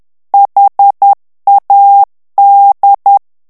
Morse code, heard as a keyed tone, about 10 wpm, 800 Hz. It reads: HAD
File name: ANGLAIS texte clair 3 vitesse 10 mots minute